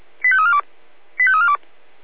Five-tone selective call (ZVEI), heard as quick runs of five tones, tones of about 70 ms. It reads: 85342 85342